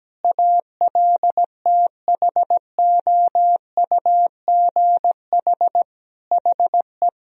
Morse code, heard as a keyed tone, about 17 wpm, 700 Hz. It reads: ALTHOUGH HE
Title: Sentences from Top 1000 Words - Review Part 3 - 17wpm